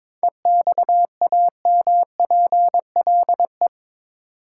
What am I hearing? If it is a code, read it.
Morse code, heard as a keyed tone, about 22 wpm, 700 Hz. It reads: EXAMPLE
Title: Sets of 2 Words - New Word Drill - 22wpm